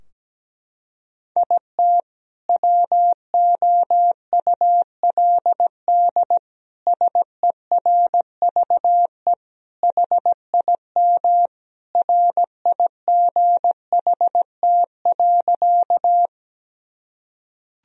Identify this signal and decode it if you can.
Morse code, heard as a keyed tone, about 17 wpm, 700 Hz. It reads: IT WOULD SERVE HIM RIGHT.
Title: Sentences from Top 1000 Words - Review Part 2 - 17wpm